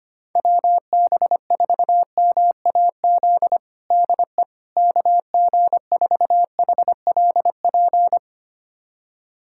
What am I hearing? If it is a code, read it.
Morse code, heard as a keyed tone, about 25 wpm, 700 Hz. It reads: WB4MAZ DE KG4HLP
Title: USA Callsign DE Callsign - 6 Characters - 25wpm